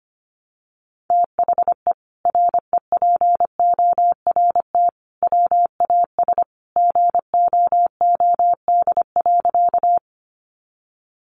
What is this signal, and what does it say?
Morse code, heard as a keyed tone, about 25 wpm, 700 Hz. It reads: THE REPORT WAS GOOD.
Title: Sentences from Top 900 Words 25wpm